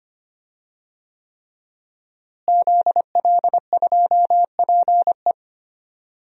Morse code, heard as a keyed tone, about 25 wpm, 700 Hz. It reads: ZL2PE
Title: DXCC-100 Call Signs - 5 Characters - Repeated 3x - 25wpm